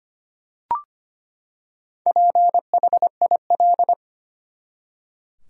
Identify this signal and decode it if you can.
Morse code, heard as a keyed tone, about 25 wpm, 700 Hz. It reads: PHIL
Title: USA Common Names - Max 4 Letters - 25wpm